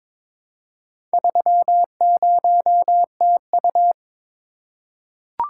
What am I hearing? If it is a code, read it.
Morse code, heard as a keyed tone, about 22 wpm, 700 Hz. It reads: 30TU
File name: practice / 22wpm / Four Letter-Number Combinations - 22wpm